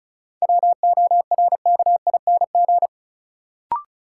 Morse code, heard as a keyed tone, about 36 wpm, 700 Hz. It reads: WORKING
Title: Sentences from Top 700 Words - New Word Drill - 35wpm